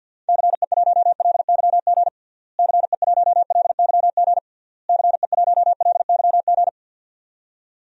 Morse code, heard as a keyed tone, about 45 wpm, 700 Hz. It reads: CE1LYG CE1LYG CE1LYG
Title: DXCC-100 Call Signs - 6 Characters - Repeated 3x - 50wpm